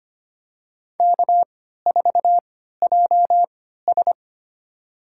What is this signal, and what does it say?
Morse code, heard as a keyed tone, about 25 wpm, spaced as for 14 wpm, 700 Hz. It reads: K4JS
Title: USA Call Signs - 4 Characters - 3x Character Spacing - 25wpm